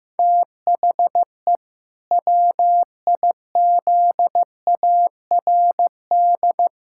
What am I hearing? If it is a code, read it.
Morse code, heard as a keyed tone, about 16 wpm, 700 Hz. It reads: THE WIZARD
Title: Pangram Challenge v2 - 15wpm